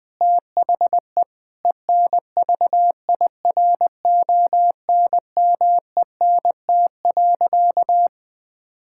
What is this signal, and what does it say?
Morse code, heard as a keyed tone, about 20 wpm, 700 Hz. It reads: THE ENVIRONMENT.
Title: Difficult Sentences with Hints - 20wpm